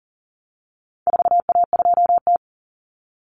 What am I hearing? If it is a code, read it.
Morse code, heard as a keyed tone, about 40 wpm, 700 Hz. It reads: 4A2T
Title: Common Cut Numbers - 4-Digit 40wpm